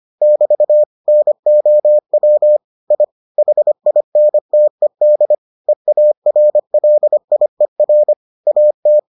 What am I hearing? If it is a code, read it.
Morse code, heard as a keyed tone, about 25 wpm, 600 Hz. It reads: <BT> NOW I HINTED EARLIER AT